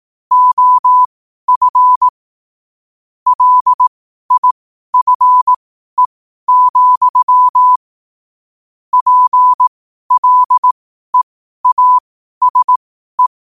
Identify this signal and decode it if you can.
Morse code, heard as a keyed tone, about 18 wpm, spaced as for 12 wpm, 1000 Hz. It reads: OF LIFE, PLEASE
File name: qotdfiles / 12wpm